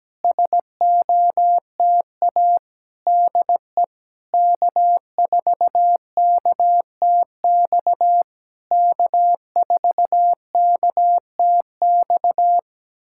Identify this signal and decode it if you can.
Morse code, heard as a keyed tone, about 16 wpm, 700 Hz. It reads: SOTA DE K4KTX K4KTX
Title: SOTA QSO Practice 17wpm